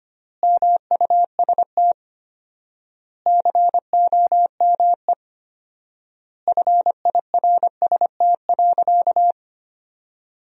Morse code, heard as a keyed tone, about 25 wpm, 700 Hz. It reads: MUST COME FIRST.
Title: 4x Word-Spacing - Sentences from Top 300 Words - 25wpm